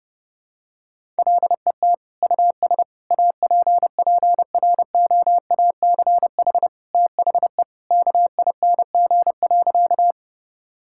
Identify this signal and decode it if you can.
Morse code, heard as a keyed tone, about 30 wpm, 700 Hz. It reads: LET US APPROACH THE KING.
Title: Sentences from Top 1000 Words - Review Part 6 - 30wpm